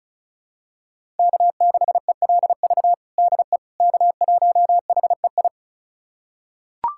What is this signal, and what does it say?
Morse code, heard as a keyed tone, about 36 wpm, 700 Hz. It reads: K6ELV DE K1HEI
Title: USA Callsign DE Callsign - 5 Characters - 35wpm